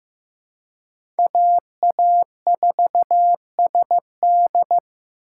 Morse code, heard as a keyed tone, about 15 wpm, 700 Hz. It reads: AA4SD
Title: USA Call Signs - 5 Characters - 4x Character Spacing - 15wpm